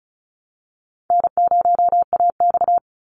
Morse code, heard as a keyed tone, about 34 wpm, 700 Hz. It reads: N0AX
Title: Call Signs Repeated 3x - 35wpm